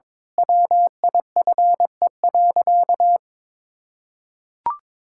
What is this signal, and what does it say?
Morse code, heard as a keyed tone, about 22 wpm, 700 Hz. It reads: WIFE.
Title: Pangram Challenge v2 - 22wpm